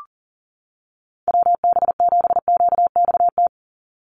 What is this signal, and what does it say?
Morse code, heard as a keyed tone, about 40 wpm, 700 Hz. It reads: WB7QXT
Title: USA Call Signs 40wpm